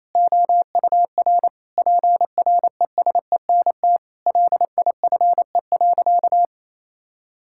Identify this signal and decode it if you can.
Morse code, heard as a keyed tone, about 28 wpm, 700 Hz. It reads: OUR PRESENT LIFE.
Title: Sentences from Top 1000 Words - Review Part 6 - 28wpm